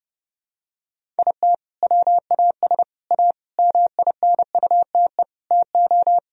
Morse code, heard as a keyed tone, about 32 wpm, 700 Hz. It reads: IT WAS A MINUTE TO
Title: Sentences from Top 1000 Words - Review Part 1 - 30wpm